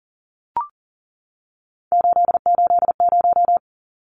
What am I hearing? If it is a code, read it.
Morse code, heard as a keyed tone, about 40 wpm, 700 Hz. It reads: 880
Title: Numbers - 3-Digit 40wpm